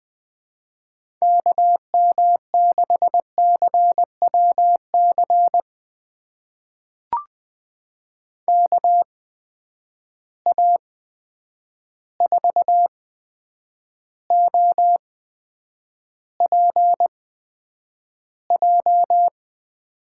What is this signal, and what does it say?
Morse code, heard as a keyed tone, about 20 wpm, spaced as for 9 wpm, 700 Hz. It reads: KM6CWC KA4OPJ
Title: USA Call Signs - 6 Characters - 8x Character Spacing - 20wpm